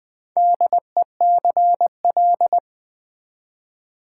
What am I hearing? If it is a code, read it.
Morse code, heard as a keyed tone, about 20 wpm, 700 Hz. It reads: DECL
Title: Build A Word - 4 Letters At A Time - 11-Letter Words - 20wpm